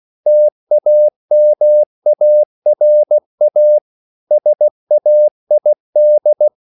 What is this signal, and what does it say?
Morse code, heard as a keyed tone, about 16 wpm, 600 Hz. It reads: TAMARA SAID